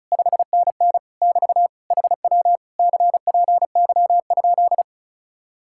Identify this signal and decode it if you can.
Morse code, heard as a keyed tone, about 46 wpm, 700 Hz. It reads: 5NN <BT> HW CPY?